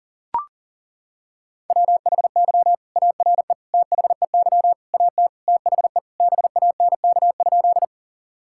Morse code, heard as a keyed tone, about 40 wpm, 690 Hz. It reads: WHY ARE THEY AT THE BANK?
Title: Sentences from Top 1000 Words - Review Part 6 - 40wpm